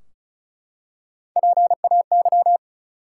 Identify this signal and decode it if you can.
Morse code, heard as a keyed tone, about 34 wpm, 700 Hz. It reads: PAY
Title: Three Letter Words - 35wpm